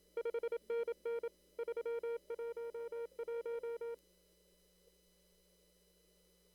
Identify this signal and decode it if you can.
Morse code, heard as a keyed tone, about 27 wpm, 470 Hz. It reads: 5NN 311
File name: CW / kt5j wpx cw-op-k5pi